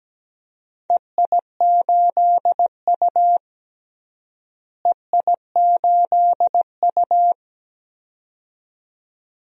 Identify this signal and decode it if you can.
Morse code, heard as a keyed tone, about 17 wpm, 700 Hz. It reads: EI8U EI8U
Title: DXCC-100 Call Signs - 4 Characters - Repeated 2x - 17wpm